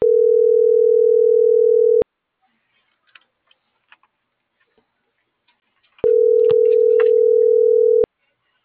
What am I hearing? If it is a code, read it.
Ringback tone (North America), 440 + 480 Hz.